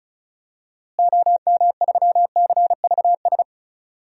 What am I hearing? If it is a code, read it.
Morse code, heard as a keyed tone, about 36 wpm, 700 Hz. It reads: OM3CVS
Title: DXCC-100 Call Signs - 6 Characters - Repeated 3x - 35wpm